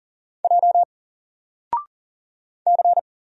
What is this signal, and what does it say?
Morse code, heard as a keyed tone, about 39 wpm, 700 Hz. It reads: J C
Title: Single Character (Letter:Number:Punctuation:Prosign) 40wpm